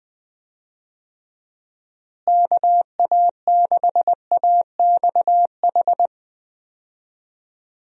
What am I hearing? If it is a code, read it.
Morse code, heard as a keyed tone, about 20 wpm, 700 Hz. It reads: KA6AXH
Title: USA Call Signs - 6 Characters - Repeated 3x - 20wpm